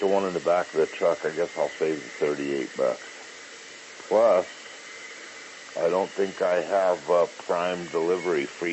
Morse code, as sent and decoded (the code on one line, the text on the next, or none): none